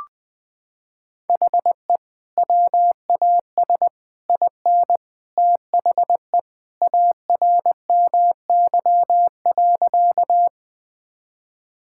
Morse code, heard as a keyed tone, about 20 wpm, 700 Hz. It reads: HE WAS IN THE ARMY.
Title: Sentences with 3-Word Phrases - 20wpm